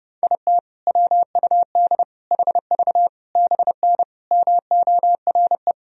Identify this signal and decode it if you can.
Morse code, heard as a keyed tone, about 30 wpm, 700 Hz. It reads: IT WUD HV BN MORE
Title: Sentences from Top 400 Words and Abbreviations 30wpm